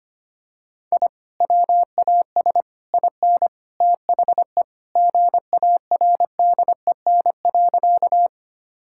Morse code, heard as a keyed tone, about 25 wpm, 700 Hz. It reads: I WAS IN THE GARDEN.
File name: practice / 25wpm / Sentences with 3-Word Phrases - 25wpm